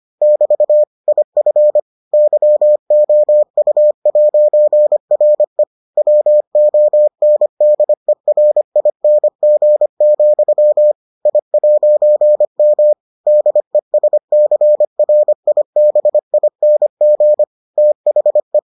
Morse code, heard as a keyed tone, about 25 wpm, 600 Hz. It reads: <BT> IF YOU'RE WONDERING, I'M DESCRIBING THE